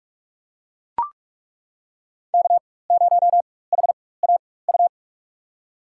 Morse code, heard as a keyed tone, about 42 wpm, spaced as for 21 wpm, 700 Hz. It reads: K0HAU
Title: USA Call Signs - 5 Characters - 4x Character Spacing - 45wpm